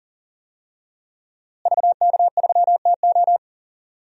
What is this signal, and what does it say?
Morse code, heard as a keyed tone, about 40 wpm, 700 Hz. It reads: VK3TO
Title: DXCC-100 Call Signs - 5 Characters - Repeated 3x - 40wpm